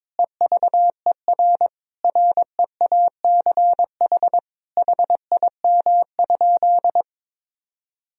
Morse code, heard as a keyed tone, about 22 wpm, 700 Hz. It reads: EVER REACH HIM?
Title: Sentences from Top 1000 Words - Review Part 2 - 22wpm